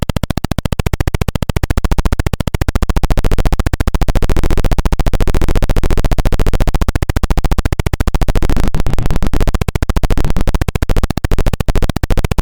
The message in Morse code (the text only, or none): none